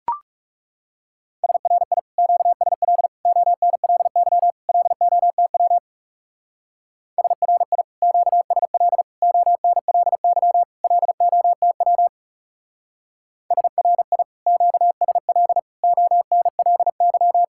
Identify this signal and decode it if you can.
Morse code, heard as a keyed tone, about 36 wpm, 700 Hz. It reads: SRI QSL ONLY LOTW SRI QSL ONLY LOTW SRI QSL ONLY
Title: Speed-Racing Common QSO Phrases Encore - 30wpm